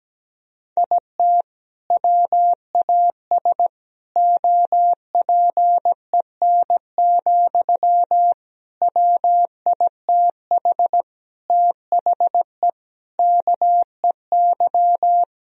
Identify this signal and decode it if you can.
Morse code, heard as a keyed tone, about 17 wpm, 700 Hz. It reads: IT WAS OPEN, WITH THE KEY
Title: Sentences from Top 1000 Words - Review Part 4 - 17wpm